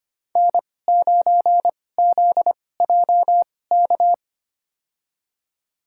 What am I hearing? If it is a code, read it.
Morse code, heard as a keyed tone, about 25 wpm, spaced as for 18 wpm, 700 Hz. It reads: N9ZJK
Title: USA Call Signs - 5 Characters - 2x Character Spacing - 25wpm